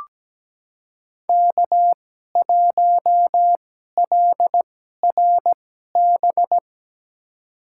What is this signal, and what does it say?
Morse code, heard as a keyed tone, about 17 wpm, spaced as for 12 wpm, 700 Hz. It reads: K1LRB
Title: USA Call Signs - 5 Characters - 2x Character Spacing - 17wpm